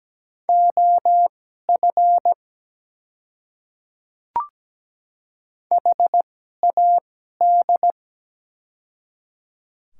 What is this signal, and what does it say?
Morse code, heard as a keyed tone, about 17 wpm, spaced as for 12 wpm, 700 Hz. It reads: OF HAD
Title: 2x Character Spacing - Top 100 Words - 17wpm